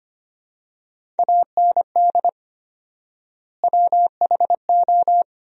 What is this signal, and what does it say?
Morse code, heard as a keyed tone, about 25 wpm, 700 Hz. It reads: AND WHO
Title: 4x Word-Spacing - Sentences from Top 100 Words - 25wpm